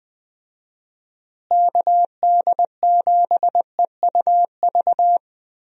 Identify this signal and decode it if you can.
Morse code, heard as a keyed tone, about 20 wpm, 700 Hz. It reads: KD7EUV
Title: USA Call Signs - 6 Characters - Repeated 3x - 20wpm